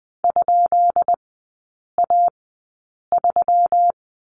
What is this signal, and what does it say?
Morse code, heard as a keyed tone, about 20 wpm, spaced as for 8 wpm, 700 Hz. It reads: ?A3
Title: 138 Lesson 23e - Letter f - Three Characters 20wpm